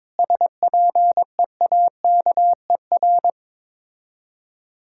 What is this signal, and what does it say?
Morse code, heard as a keyed tone, about 22 wpm, 700 Hz. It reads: SPEAKER
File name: Difficult Words with Hints - 22wpm